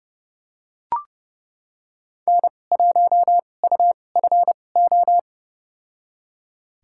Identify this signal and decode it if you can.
Morse code, heard as a keyed tone, about 30 wpm, spaced as for 22 wpm, 700 Hz. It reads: N1UFO
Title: USA Call Signs - 5 Characters - 2x Character Spacing - 30wpm